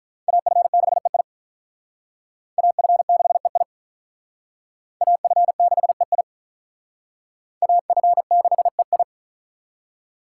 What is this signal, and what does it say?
Morse code, heard as a keyed tone, about 44 wpm, 700 Hz. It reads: AF6EI AF6EI AF6EI AF6EI
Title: Speed Racing - USA 2x2 Call Signs 35wpm